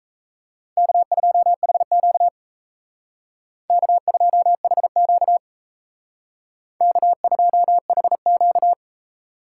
Morse code, heard as a keyed tone, about 35 wpm, 700 Hz. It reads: K2HQ K2HQ K2HQ